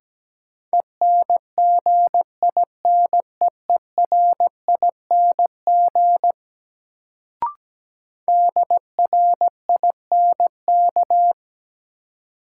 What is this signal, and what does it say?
Morse code, heard as a keyed tone, about 17 wpm, 700 Hz. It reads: ENGINEERING DRINK
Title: Sentences from Top 1500 Words - New Word Drill - 17wpm